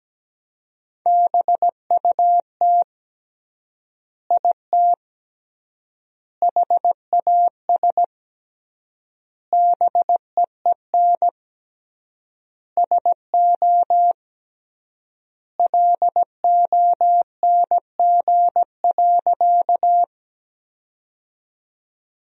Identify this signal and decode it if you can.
Morse code, heard as a keyed tone, about 17 wpm, 700 Hz. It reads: BUT IT HAS BEEN SO LONG.
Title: 3x Word-Spacing - Sentences from Top 100 Words - 17wpm